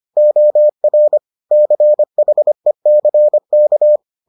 Morse code, heard as a keyed tone, about 25 wpm, 600 Hz. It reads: OR CHECK